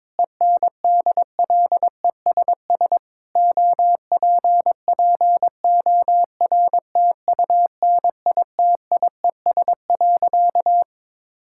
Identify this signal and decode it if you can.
Morse code, heard as a keyed tone, about 22 wpm, 700 Hz. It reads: ENDLESS OPPORTUNITIES.
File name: Difficult Sentences with Hints - 22wpm